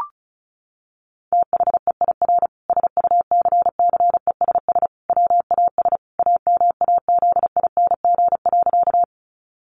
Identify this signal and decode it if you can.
Morse code, heard as a keyed tone, about 34 wpm, 700 Hz. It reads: THEIR SUCCESS WAS AMAZING.
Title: Sentences from Top 1500 Words 35wpm